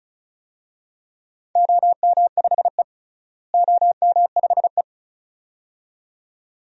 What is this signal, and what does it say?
Morse code, heard as a keyed tone, about 36 wpm, 700 Hz. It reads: OM5E OM5E
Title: DXCC-100 Call Signs - 4 Characters - Repeated 2x - 35wpm